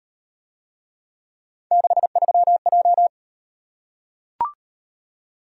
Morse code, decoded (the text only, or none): 63J